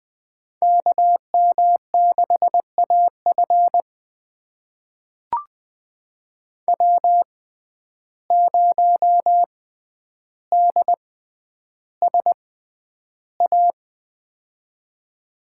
Morse code, heard as a keyed tone, about 20 wpm, spaced as for 10 wpm, 700 Hz. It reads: KM6AF W0DSA
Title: USA Call Signs - 5 Characters - 6x Character Spacing - 20wpm